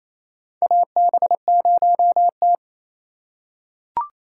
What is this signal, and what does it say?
Morse code, heard as a keyed tone, about 28 wpm, 700 Hz. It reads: AB0T
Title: USA Call Signs - 4 Characters - 3x Character Spacing - 28wpm